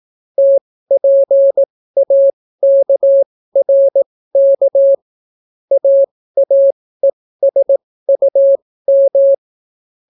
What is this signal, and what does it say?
Morse code, heard as a keyed tone, about 18 wpm, spaced as for 14 wpm, 550 Hz. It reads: TPAKRK AAESUM